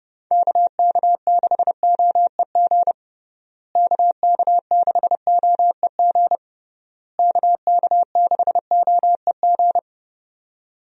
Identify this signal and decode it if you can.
Morse code, heard as a keyed tone, about 30 wpm, 700 Hz. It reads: KK6OEG KK6OEG KK6OEG
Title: USA Call Signs - 6 Characters - Repeated 3x - 30wpm